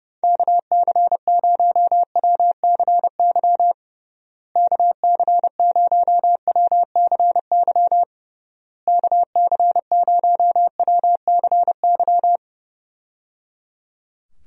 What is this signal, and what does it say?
Morse code, heard as a keyed tone, about 30 wpm, 700 Hz. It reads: KC0WCY KC0WCY KC0WCY
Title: USA Call Signs - 6 Characters - Repeated 3x - 30wpm